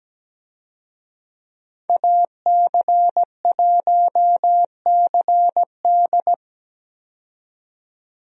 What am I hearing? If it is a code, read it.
Morse code, heard as a keyed tone, about 17 wpm, 700 Hz. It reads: AC1CD
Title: USA Call Signs - 5 Characters - 17wpm